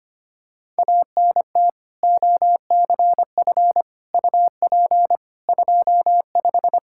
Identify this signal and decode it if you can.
Morse code, heard as a keyed tone, about 25 wpm, 700 Hz. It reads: ANT OCF UP 25
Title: Practice QSO - Exchange 2 - Info - 25wpm